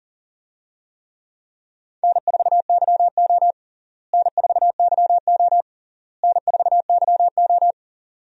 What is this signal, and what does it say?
Morse code, heard as a keyed tone, about 40 wpm, 700 Hz. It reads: N4YO N4YO N4YO
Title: USA Call Signs - 4 Characters - Repeated 3x - 40wpm